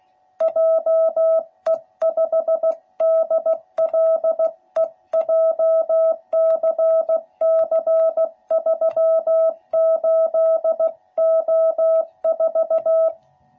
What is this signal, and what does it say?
Morse code, heard as a keyed tone, about 15 wpm, 660 Hz. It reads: JE5DLE JCC38O4
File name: jh8cbh / 2021cw / cw12b